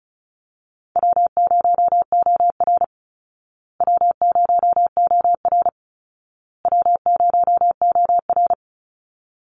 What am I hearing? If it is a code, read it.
Morse code, heard as a keyed tone, about 34 wpm, 700 Hz. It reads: W0OR W0OR W0OR
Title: Easy Call Signs Repeated 3x - 35wpm